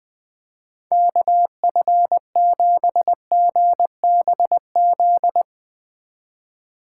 Morse code, heard as a keyed tone, about 20 wpm, 700 Hz. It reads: KF7GBZ
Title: USA Call Signs - 6 Characters - 4x Character Spacing - 20wpm